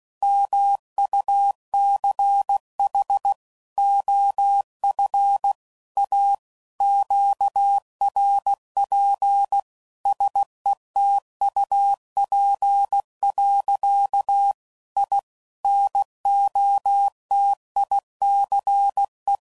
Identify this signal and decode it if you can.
Morse code, heard as a keyed tone, about 16 wpm, 800 Hz. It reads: MUCH OF A QRP SETUP. I NOTICE
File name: CW Code Groups @15WPM / QSOs / WD0BGP Spaced